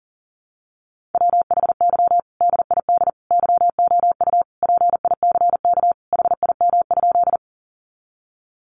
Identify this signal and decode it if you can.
Morse code, heard as a keyed tone, about 40 wpm, 700 Hz. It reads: WHY DID YOU PICK HIM?
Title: Sentences from Top 1000 Words 40wpm